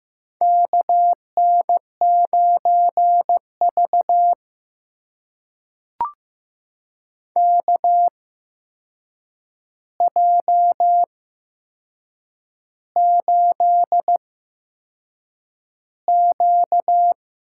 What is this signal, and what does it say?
Morse code, heard as a keyed tone, about 15 wpm, spaced as for 6 wpm, 700 Hz. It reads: KN9V KJ8Q
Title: USA Call Signs - 4 Characters - 8x Character Spacing - 15wpm